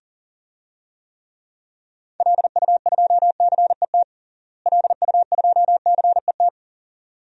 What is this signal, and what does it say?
Morse code, heard as a keyed tone, about 37 wpm, 700 Hz. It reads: LU2CET LU2CET
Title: DXCC-100 Call Signs - 6 Characters - Repeated 2x - 40wpm